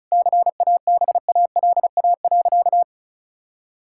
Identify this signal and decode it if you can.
Morse code, heard as a keyed tone, about 36 wpm, 700 Hz. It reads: CABALA.
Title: Pangram Challenge v2 - 35wpm